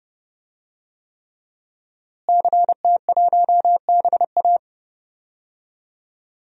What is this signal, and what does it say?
Morse code, heard as a keyed tone, about 29 wpm, 700 Hz. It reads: CT1BA
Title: DXCC-100 Call Signs - 5 Characters - Repeated 3x - 30wpm